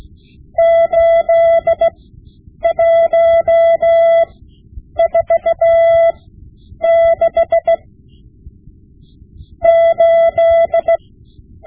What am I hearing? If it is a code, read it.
Morse code, heard as a keyed tone, about 16 wpm, spaced as for 7 wpm, 670 Hz. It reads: 81468
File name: Morse Lessons / Morse Lessons / Tape-5B